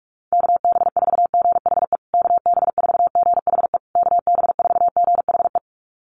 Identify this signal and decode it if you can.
Morse code, heard as a keyed tone, about 46 wpm, 700 Hz. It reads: KB4GHE KB4GHE KB4GHE
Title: USA Call Signs Repeated 3x 45wpm